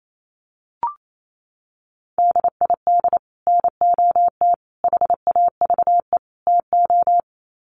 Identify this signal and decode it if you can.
Morse code, heard as a keyed tone, about 28 wpm, 700 Hz. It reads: DID NOT HAVE TO
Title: Sets of 4 Words 28wpm